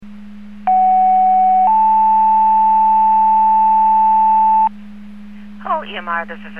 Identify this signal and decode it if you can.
Two-tone page, heard as a one-second tone, then a three-second tone, about 750 then 880 Hz, with speech. It is heard from 0.7 to 4.7 s.